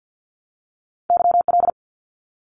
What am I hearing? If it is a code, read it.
Morse code, heard as a keyed tone, about 50 wpm, 700 Hz. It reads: YL
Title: Abbreviations Repeated 3x - 50wpm